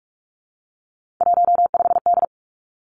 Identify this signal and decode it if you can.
Morse code, heard as a keyed tone, about 46 wpm, 700 Hz. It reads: 15D
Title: Three Challenging Letter-Number Combinations 45wpm